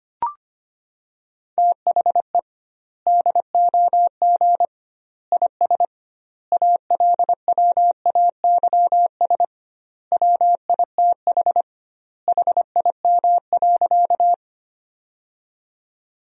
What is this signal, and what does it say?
Morse code, heard as a keyed tone, about 25 wpm, 700 Hz. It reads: THE DOG IS ALWAYS WITH HIM.
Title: 2x Word-Spacing - Sentences from Top 400 Words - 25wpm